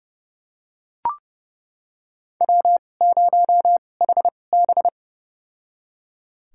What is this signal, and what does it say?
Morse code, heard as a keyed tone, about 31 wpm, spaced as for 22 wpm, 700 Hz. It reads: W0HB
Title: USA Call Signs - 4 Characters - 2x Character Spacing - 30wpm